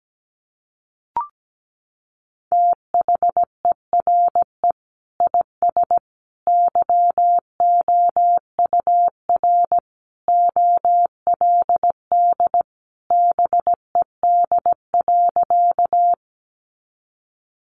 Morse code, heard as a keyed tone, about 17 wpm, 700 Hz. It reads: THERE IS YOUR OLD BED.
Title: Sentences from Top 500 Words 17wpm